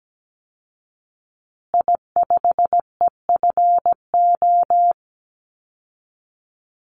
Morse code, heard as a keyed tone, about 17 wpm, 700 Hz. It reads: I5EFO
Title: Call Signs Repeated 3x - 17wpm